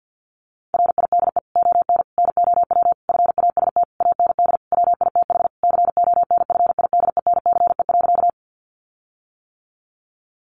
Morse code, heard as a keyed tone, about 50 wpm, 700 Hz. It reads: RIDE ON NOW FAST AND WITH CONFIDENCE.
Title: Sentences from Top 1500 Words 50wpm